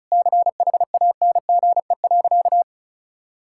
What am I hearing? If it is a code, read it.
Morse code, heard as a keyed tone, about 36 wpm, 700 Hz. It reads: CHANGE.
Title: Sentences from Top 1000 Words - Review Part 5 - 35wpm